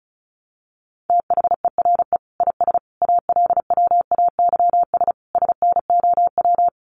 Morse code, heard as a keyed tone, about 34 wpm, 700 Hz. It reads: THERE IS ALWAYS SNOW
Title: Sentences from Top 1000 Words 35wpm